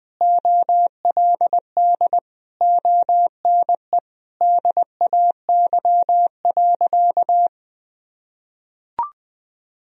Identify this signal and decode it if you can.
Morse code, heard as a keyed tone, about 20 wpm, 700 Hz. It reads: OLD ONE DAY.
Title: Sentences from Top 1000 Words - Review Part 1 - 20wpm